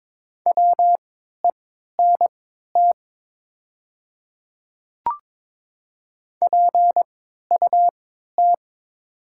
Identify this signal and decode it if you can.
Morse code, heard as a keyed tone, about 22 wpm, spaced as for 13 wpm, 700 Hz. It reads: WENT PUT
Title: 3x Character Spacing - Top 200 Words - 22wpm